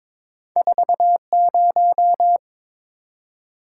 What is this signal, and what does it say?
Morse code, heard as a keyed tone, about 22 wpm, 700 Hz. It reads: 40
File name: practice / 22wpm / Two Letter-Number Combinations - 22wpm